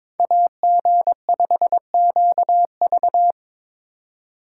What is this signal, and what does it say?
Morse code, heard as a keyed tone, about 26 wpm, 700 Hz. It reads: AG5QV
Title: Speed Racing - USA Call Signs 22wpm